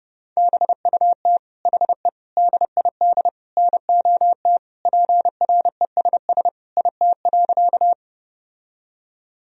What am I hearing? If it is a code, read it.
Morse code, heard as a keyed tone, about 30 wpm, 700 Hz. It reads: BUT HE DID NOT PRESS IT.
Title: Sentences from Top 1000 Words - Review Part 1 - 30wpm